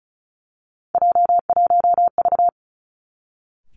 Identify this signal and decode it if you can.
Morse code, heard as a keyed tone, about 36 wpm, 700 Hz. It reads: J1V
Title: Three Challenging Letter-Number Combinations 35wpm